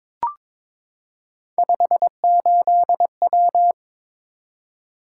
Morse code, heard as a keyed tone, about 22 wpm, 700 Hz. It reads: 58W
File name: practice / 22wpm / Three Letter-Number Combinations - 22wpm